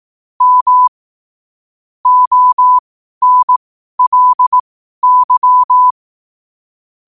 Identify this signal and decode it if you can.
Morse code, heard as a keyed tone, about 18 wpm, spaced as for 12 wpm, 1000 Hz. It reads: M ONLY
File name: qotdfiles / 12wpm